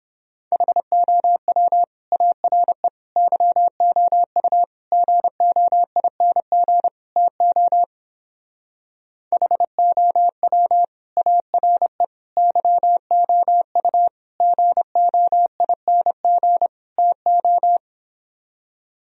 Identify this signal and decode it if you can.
Morse code, heard as a keyed tone, about 31 wpm, 700 Hz. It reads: HOW ARE YOU GOING TO HOW ARE YOU GOING TO